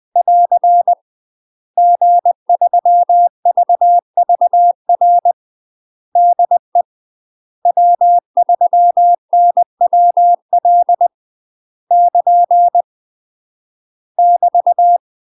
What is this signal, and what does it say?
Morse code, heard as a keyed tone, about 20 wpm, 700 Hz. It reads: <AR> G3VVR DE W3NWL <KN> <BT>
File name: UK MASTER QSOs TRAINING / 20WPM / UK MASTER QSOs 0063